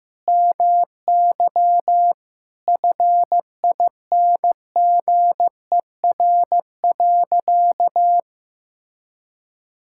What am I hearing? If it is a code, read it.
Morse code, heard as a keyed tone, about 15 wpm, 700 Hz. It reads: MY FINGER.
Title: Sentences from Top 1600 Words 15wpm